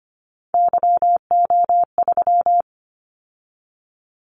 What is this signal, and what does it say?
Morse code, heard as a keyed tone, about 25 wpm, 700 Hz. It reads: YO3
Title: Callsign Prefixes for Top 25 DXCC - 25wpm